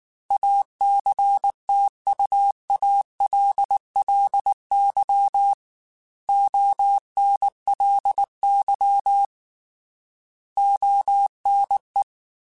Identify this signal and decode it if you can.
Morse code, heard as a keyed tone, about 19 wpm, 800 Hz. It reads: ACTUALLY ONLY ONE
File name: Technical Subjects / Technical Subjects-mp3s / AND Gate @18